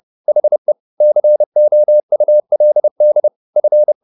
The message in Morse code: HE COULD F

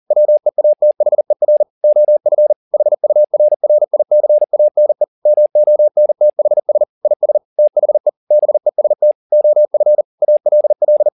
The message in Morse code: WEATHER OF HURRICANE MONTHS IS THE BEST OF ALL

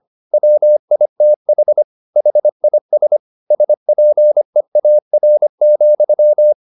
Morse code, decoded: WITH HIS SPEAR,

